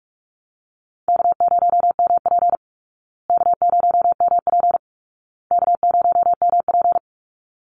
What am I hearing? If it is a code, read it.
Morse code, heard as a keyed tone, about 44 wpm, 700 Hz. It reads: K0MP K0MP K0MP
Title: Call Signs Repeated 2x - 45wpm